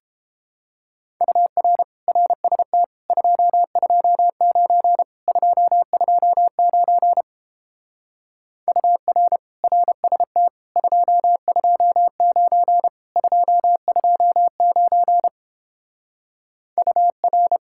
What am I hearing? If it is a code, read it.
Morse code, heard as a keyed tone, about 31 wpm, 700 Hz. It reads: UR RST 229 229 UR RST 229 229 UR